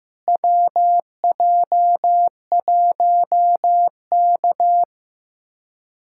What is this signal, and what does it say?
Morse code, heard as a keyed tone, about 15 wpm, 700 Hz. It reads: WJ1K
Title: USA Callsign DE Callsign - 4 Characters - 15wpm